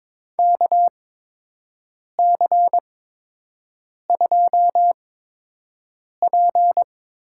Morse code, heard as a keyed tone, about 22 wpm, spaced as for 6 wpm, 700 Hz. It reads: KC2P
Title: USA Call Signs - 6 Characters - 8x Character Spacing - 22wpm